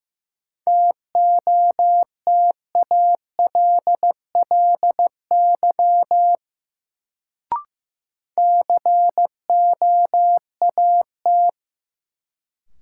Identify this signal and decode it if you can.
Morse code, heard as a keyed tone, about 16 wpm, 700 Hz. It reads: TOTALLY COAT
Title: Sentences from Top 1600 Words - New Word Drill Alt - 15wpm